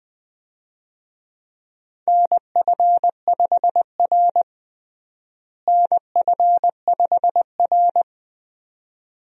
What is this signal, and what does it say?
Morse code, heard as a keyed tone, about 20 wpm, 700 Hz. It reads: NF5R NF5R
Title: USA Call Signs - 4 Characters - Repeated 2x - 20wpm